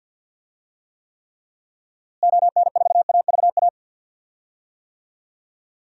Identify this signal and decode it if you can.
Morse code, heard as a keyed tone, about 52 wpm, 700 Hz. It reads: ON4AVA
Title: DXCC-100 Call Signs - 6 Characters - Repeated 3x - 50wpm